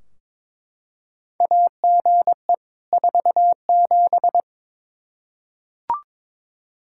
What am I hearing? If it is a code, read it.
Morse code, heard as a keyed tone, about 22 wpm, 700 Hz. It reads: AGE 47
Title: Practice QSO - Exchange 3 - More Info - 22wpm